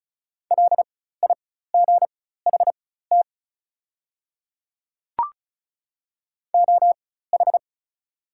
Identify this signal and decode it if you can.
Morse code, heard as a keyed tone, about 35 wpm, spaced as for 16 wpm, 690 Hz. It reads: LIGHT OH
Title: 4x Character Spacing - Top 500 Words - 35wpm